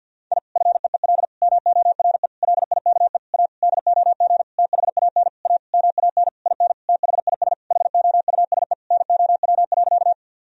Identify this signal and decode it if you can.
Morse code, heard as a keyed tone, about 45 wpm, 700 Hz. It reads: I FEEL MORE LIKE A DOG THAN A MAN IN THIS HOUSE NOW.
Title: Sentences from Top 1000 Words - Review Part 2 - 50wpm